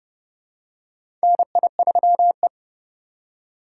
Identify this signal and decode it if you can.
Morse code, heard as a keyed tone, about 29 wpm, 700 Hz. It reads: NI3E
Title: USA Call Signs - 4 Characters - 2x Character Spacing - 30wpm